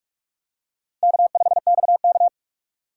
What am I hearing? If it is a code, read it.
Morse code, heard as a keyed tone, about 42 wpm, 700 Hz. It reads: K5XK
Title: K1USN SST QSO Practice - 45wpm